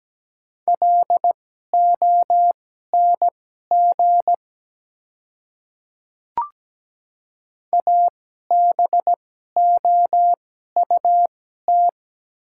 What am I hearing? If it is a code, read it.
Morse code, heard as a keyed tone, about 17 wpm, spaced as for 12 wpm, 700 Hz. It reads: LONG ABOUT
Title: 2x Character Spacing - Top 100 Words - 17wpm